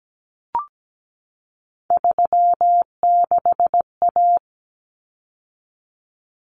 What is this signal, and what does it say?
Morse code, heard as a keyed tone, about 17 wpm, 700 Hz. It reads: 36A
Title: Common Cut Numbers - 3-Digit 17wpm